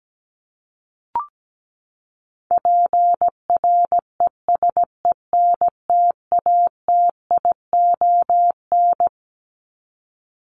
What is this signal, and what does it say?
Morse code, heard as a keyed tone, about 17 wpm, 700 Hz. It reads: PRESENTATION
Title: Top 1501-1750 Words 17wpm